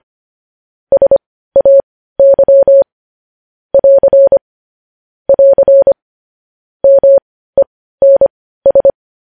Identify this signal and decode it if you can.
Morse code, heard as a keyed tone, about 25 wpm, spaced as for 15 wpm, 550 Hz. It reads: SAY <AR> <AR> MENS
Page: News Headlines in Morse Code at 15 WPM